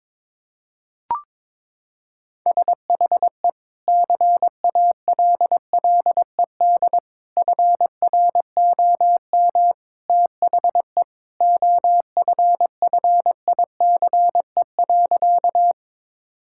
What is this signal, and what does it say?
Morse code, heard as a keyed tone, about 22 wpm, 700 Hz. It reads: SHE CALLED FROM THE OFFICE.
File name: practice / 22wpm / Sentences with 2-Word Phrases - 22wpm